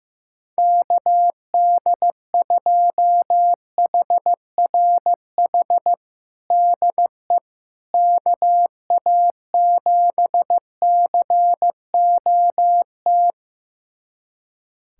Morse code, heard as a keyed tone, about 15 wpm, 700 Hz. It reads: KD2HRH DE KA7COT
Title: USA Callsign DE Callsign 15wpm